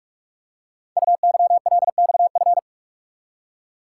Morse code, heard as a keyed tone, about 44 wpm, 690 Hz. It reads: UYLXF
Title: Code Groups - 5 Letters - 45wpm